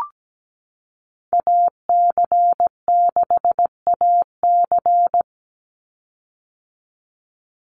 Morse code, heard as a keyed tone, about 17 wpm, 700 Hz. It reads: AC6AC
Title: Call Signs - 17wpm